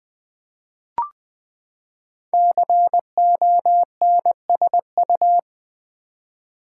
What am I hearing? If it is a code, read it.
Morse code, heard as a keyed tone, about 20 wpm, 700 Hz. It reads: CONSU
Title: Build A Word - 5 Letters At A Time - 12-Letter Words - 20wpm